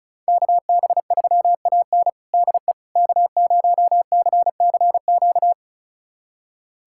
Morse code, heard as a keyed tone, about 34 wpm, 700 Hz. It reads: KB3AN DE K0CCQ
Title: USA Callsign DE Callsign - 5 Characters - 35wpm